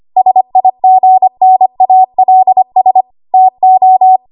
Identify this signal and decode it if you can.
Morse code, heard as a keyed tone, about 25 wpm, 750 Hz. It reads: SIGNALS TO
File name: Archive / 25 WPM / 25WPM